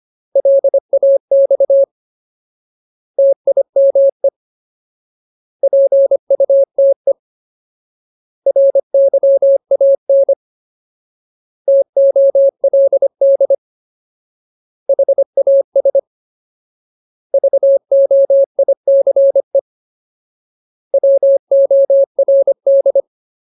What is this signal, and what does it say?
Morse code, heard as a keyed tone, about 25 wpm, 550 Hz. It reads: LAX TIME PUTE RYAN TOLD HAS VOICE WORD
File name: Ultimate Word List / 25WPM Ultimate